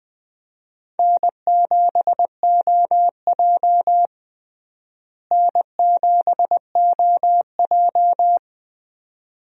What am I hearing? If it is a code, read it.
Morse code, heard as a keyed tone, about 20 wpm, 700 Hz. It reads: N7OJ N7OJ
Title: USA Call Signs - 4 Characters - Repeated 2x - 20wpm